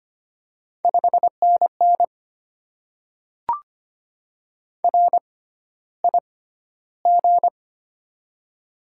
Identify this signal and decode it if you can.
Morse code, heard as a keyed tone, about 25 wpm, spaced as for 13 wpm, 700 Hz. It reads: 5NN RIG
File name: Top 20 QSO Elements - 6x Character Spacing - 25wpm